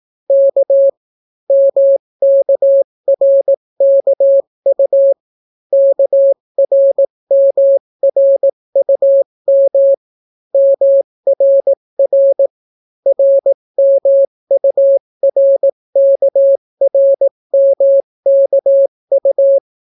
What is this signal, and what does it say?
Morse code, heard as a keyed tone, about 18 wpm, 550 Hz. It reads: K MKRKU KRMRUM MRR RMURKRMKU